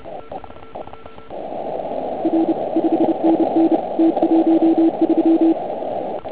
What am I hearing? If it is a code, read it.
Morse code, heard as a keyed tone, about 30 wpm, 330 Hz. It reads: R 5NN T13
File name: Archiv / CUC149 / OK1IF CUC149